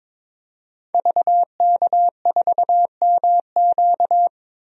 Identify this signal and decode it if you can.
Morse code, heard as a keyed tone, about 22 wpm, 700 Hz. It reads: VK4MQ
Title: DXCC-100 Call Signs - 5 Characters - 22wpm